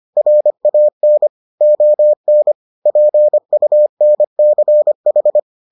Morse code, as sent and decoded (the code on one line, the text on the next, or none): .-. .- -. / --- -. / .--. ..- -. -.-. ....
RAN ON PUNCH